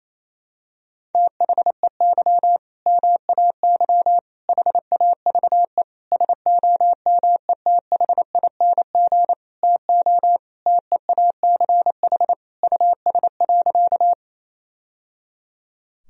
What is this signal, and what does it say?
Morse code, heard as a keyed tone, about 28 wpm, 700 Hz. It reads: THEY MAY HAVE SOMETHING TO TEACH US.
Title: Sentences from Top 1000 Words - Review Part 2 - 28wpm